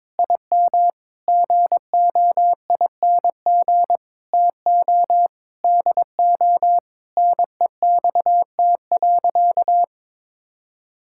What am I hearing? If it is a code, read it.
Morse code, heard as a keyed tone, about 22 wpm, 700 Hz. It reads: IM GOING TO DO NEXT.
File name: practice / 22wpm / Sentences with 8-Word Phrases - 22wpm